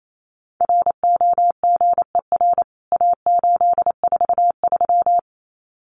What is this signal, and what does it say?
Morse code, heard as a keyed tone, about 28 wpm, 700 Hz. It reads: ROGER A843
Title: CWT Exchange 28wpm